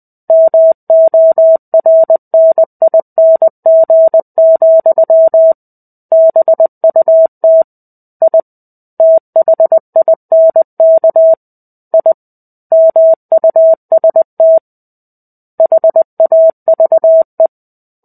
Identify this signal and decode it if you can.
Morse code, heard as a keyed tone, about 20 wpm, 650 Hz. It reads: MORNING, BUT I THINK I MUST HAVE
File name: 05-Chapter-05 20 WPM